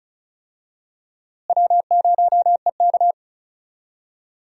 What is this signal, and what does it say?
Morse code, heard as a keyed tone, about 36 wpm, 700 Hz. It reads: W0EK
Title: USA Call Signs - 4 Characters - Repeated 3x - 35wpm